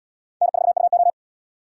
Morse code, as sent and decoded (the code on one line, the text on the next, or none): -. ....- ..- ---
N4UO